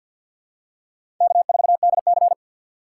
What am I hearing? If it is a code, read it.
Morse code, heard as a keyed tone, about 53 wpm, 700 Hz. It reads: K4DC
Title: USA Call Signs - 4 Characters - 3x Character Spacing - 50wpm